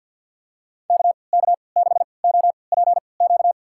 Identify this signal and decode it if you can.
Morse code, heard as a keyed tone, about 47 wpm, spaced as for 28 wpm, 700 Hz. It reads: KK6OPQ
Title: USA Call Signs - 6 Characters - 3x Character Spacing - 50wpm